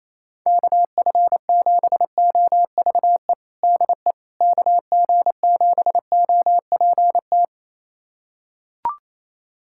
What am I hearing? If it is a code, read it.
Morse code, heard as a keyed tone, about 28 wpm, 700 Hz. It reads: KF7OVE DE KG7OPT
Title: USA Callsign DE Callsign - 6 Characters - 28wpm